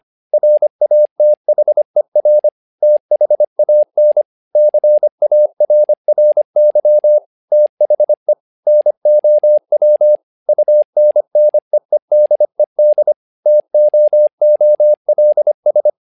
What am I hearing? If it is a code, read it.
Morse code, heard as a keyed tone, about 25 wpm, 600 Hz. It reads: RATHER THAN CARRY THE NOW UNNEEDED TOOLS
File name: learning center / tower trouble / tower trouble 25wpm